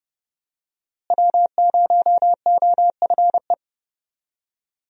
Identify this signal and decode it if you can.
Morse code, heard as a keyed tone, about 32 wpm, 700 Hz. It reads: W0OFE
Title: USA Call Signs - 5 Characters - Repeated 3x - 30wpm